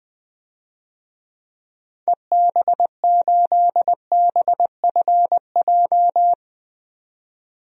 Morse code, heard as a keyed tone, about 21 wpm, 700 Hz. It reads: EB8BFJ
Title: DXCC-100 Call Signs - 6 Characters - 20wpm